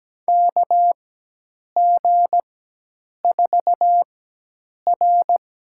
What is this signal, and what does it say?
Morse code, heard as a keyed tone, about 17 wpm, spaced as for 8 wpm, 700 Hz. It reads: KG4R
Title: USA Call Signs - 6 Characters - 4x Character Spacing - 17wpm